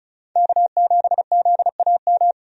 Morse code, heard as a keyed tone, about 34 wpm, 700 Hz. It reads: K7ZAM
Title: USA Call Signs - 5 Characters - 35wpm